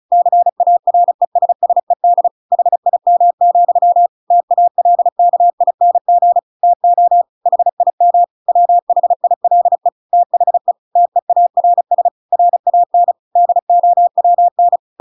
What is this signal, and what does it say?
Morse code, heard as a keyed tone, about 34 wpm, 700 Hz. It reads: CARESSED HIM, TALKING TO HIM WHILE THE TEARS RAN DOWN
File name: Pinocchio / Pinocchio ch7 35 35 ik1hge